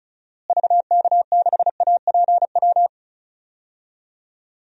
Morse code, heard as a keyed tone, about 36 wpm, 700 Hz. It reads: VK6APW
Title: DXCC-100 Call Signs - 6 Characters - Repeated 3x - 35wpm